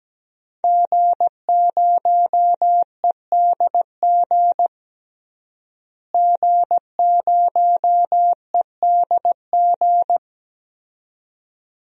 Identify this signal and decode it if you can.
Morse code, heard as a keyed tone, about 17 wpm, 700 Hz. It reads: G0EDG G0EDG
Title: DXCC-100 Call Signs - 5 Characters - Repeated 2x - 17wpm